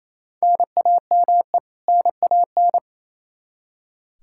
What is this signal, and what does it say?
Morse code, heard as a keyed tone, about 28 wpm, 700 Hz. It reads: NAME NAN
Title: Practice QSO - Exchange 1 - Beginning - 28wpm